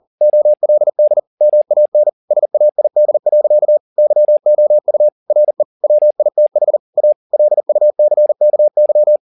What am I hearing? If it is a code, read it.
Morse code, heard as a keyed tone, about 40 wpm, 600 Hz. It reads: OLD MAN SAID. YOU RE WITH A LUCKY
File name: the old man and the sea 1 40wpm